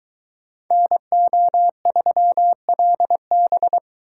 Morse code, heard as a keyed tone, about 22 wpm, 700 Hz. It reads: NO3LB